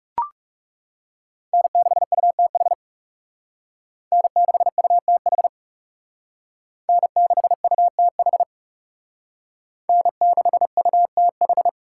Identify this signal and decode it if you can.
Morse code, heard as a keyed tone, about 37 wpm, 700 Hz. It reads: N6UTH N6UTH N6UTH N6UTH
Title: Speed Racing - USA 1x3 Call Signs 30wpm